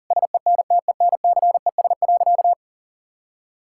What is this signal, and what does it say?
Morse code, heard as a keyed tone, about 40 wpm, 700 Hz. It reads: SENTENCES.
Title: Pangram Challenge v2 - 40wpm